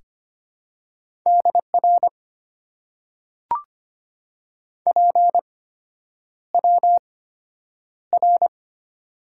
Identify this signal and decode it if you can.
Morse code, heard as a keyed tone, about 25 wpm, spaced as for 9 wpm, 690 Hz. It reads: DR PWR
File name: Top 90 QSO Elements - 8x Character Spacing - 25wpm